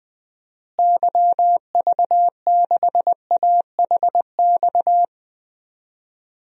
Morse code, heard as a keyed tone, about 20 wpm, 700 Hz. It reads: YV6AHX
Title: DXCC-100 Call Signs - 6 Characters - Repeated 3x - 20wpm